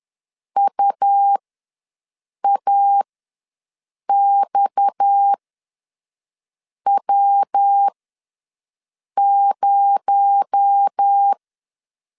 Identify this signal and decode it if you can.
Morse code, heard as a keyed tone, about 10 wpm, spaced as for 5 wpm, 800 Hz. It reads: UAXW0
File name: LEZIONI / 6 PKX0 / 6 PKX0 ws 10wpm 4220